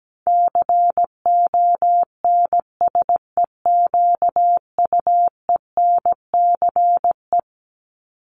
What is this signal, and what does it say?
Morse code, heard as a keyed tone, about 17 wpm, 700 Hz. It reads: CONSEQUENCE
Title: LY Suffix - Top 100 Words with Base Words 17wpm